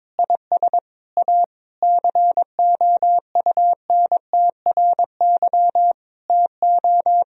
Morse code, heard as a keyed tone, about 21 wpm, 700 Hz. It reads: IS A COUNTRY TO
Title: Sentences from Top 1000 Words - Review Part 1 - 22wpm